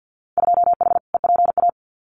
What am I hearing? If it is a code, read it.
Morse code, heard as a keyed tone, about 50 wpm, 700 Hz. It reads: 2H EPA
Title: Winter Field Day Exchanges 50wpm